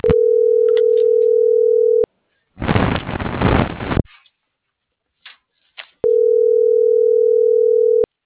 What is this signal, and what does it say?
Ringback tone (North America), 440 + 480 Hz.
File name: CENTRAL TERRESTRE / out claudiacruz - .76280